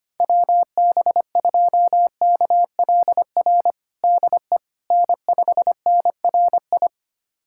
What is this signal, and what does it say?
Morse code, heard as a keyed tone, about 25 wpm, 700 Hz. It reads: WB2KLR DE N5NRI
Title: USA Callsign DE Callsign 25wpm